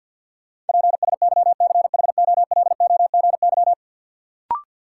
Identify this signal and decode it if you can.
Morse code, heard as a keyed tone, about 49 wpm, 700 Hz. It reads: PSYCHOLOGY
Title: Sentences from Top 1400 Words - New Word Drill Alt - 50wpm